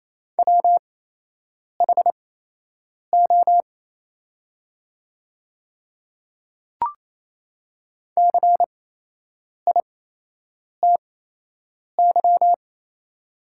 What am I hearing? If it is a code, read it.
Morse code, heard as a keyed tone, about 28 wpm, spaced as for 8 wpm, 700 Hz. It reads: WHO CITY
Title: 8x Character Spacing - Top 500 Words - 28wpm